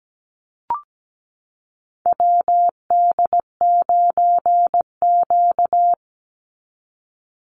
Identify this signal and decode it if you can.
Morse code, heard as a keyed tone, about 17 wpm, 700 Hz. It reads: WD9Q
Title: Call Signs Repeated 3x - 17wpm